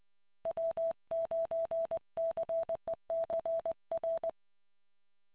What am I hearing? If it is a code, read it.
Morse code, heard as a keyed tone, about 20 wpm, 660 Hz. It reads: W9CEC R